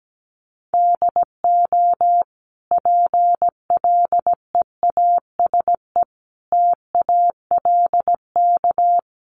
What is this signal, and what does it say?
Morse code, heard as a keyed tone, about 17 wpm, 700 Hz. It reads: DO PLEASE TALK
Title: Sentences from Top 1500 Words 17wpm